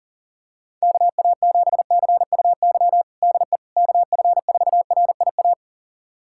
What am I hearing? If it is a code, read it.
Morse code, heard as a keyed tone, about 40 wpm, 700 Hz. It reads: KA7CUY DE KF4RIA
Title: USA Callsign DE Callsign 40wpm